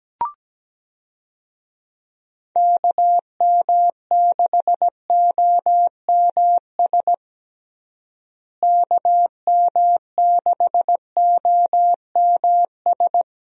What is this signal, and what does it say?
Morse code, heard as a keyed tone, about 17 wpm, 700 Hz. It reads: KM6OMS KM6OMS
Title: USA Call Signs - 6 Characters - Repeated 2x - 17wpm